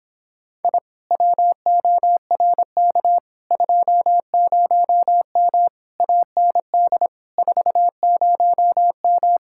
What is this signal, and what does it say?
Morse code, heard as a keyed tone, about 26 wpm, 700 Hz. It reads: I WORK 20M AND 40M